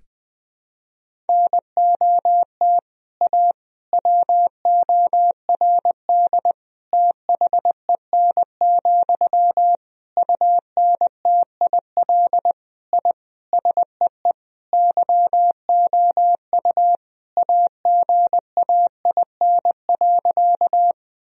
Morse code, heard as a keyed tone, about 20 wpm, 700 Hz. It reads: NOT A WORD THEN, UNTIL I SEE YOU AGAIN.
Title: Sentences from Top 1000 Words - Review Part 1 - 20wpm